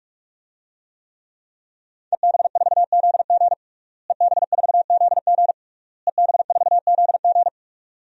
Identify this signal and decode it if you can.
Morse code, heard as a keyed tone, about 46 wpm, 700 Hz. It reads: EB4ZG EB4ZG EB4ZG
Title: DXCC-100 Call Signs - 5 Characters - Repeated 3x - 45wpm